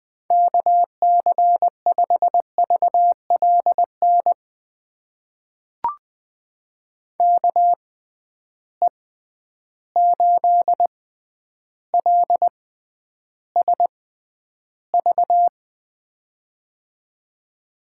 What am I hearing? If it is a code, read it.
Morse code, heard as a keyed tone, about 20 wpm, spaced as for 10 wpm, 700 Hz. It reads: KC5VLN KE8LSV
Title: USA Call Signs - 6 Characters - 6x Character Spacing - 20wpm